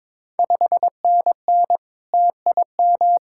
Morse code, heard as a keyed tone, about 22 wpm, 700 Hz. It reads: 5NN TIM
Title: Ontario QSO Party Exchange 22wpm